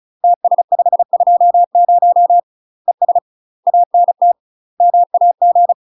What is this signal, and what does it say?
Morse code, heard as a keyed tone, about 34 wpm, 700 Hz. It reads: TS520 ES ANT MAG
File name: UK MASTER QSOs TRAINING / 35WPM / UK MASTER QSOs 0043